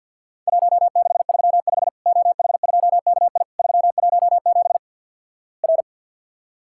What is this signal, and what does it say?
Morse code, heard as a keyed tone, about 50 wpm, 700 Hz. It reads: 1635 OH2KI 317 R
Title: Worked All Europe DX Contest - QTC Exchange - 50wpm